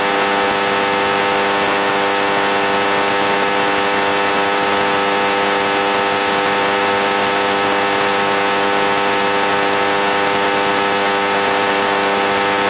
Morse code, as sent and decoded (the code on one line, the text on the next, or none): none